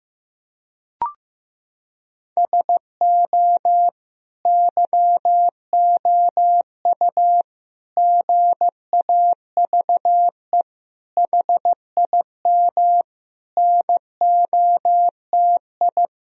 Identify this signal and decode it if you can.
Morse code, heard as a keyed tone, about 16 wpm, 700 Hz. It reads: SO YOU GAVE HIM NOTI
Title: Sentences from Top 1000 Words - Review Part 2 - 15wpm